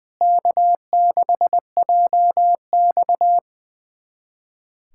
Morse code, heard as a keyed tone, about 20 wpm, 700 Hz. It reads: K6JX
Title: USA Callsign DE Callsign 20wpm